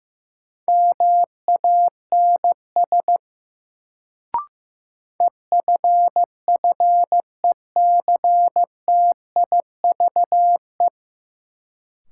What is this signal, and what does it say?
Morse code, heard as a keyed tone, about 15 wpm, 700 Hz. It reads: MANS EFFECTIVE
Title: Sentences from Top 1000 Words - Review Part 6 - New Word Drill - 15wpm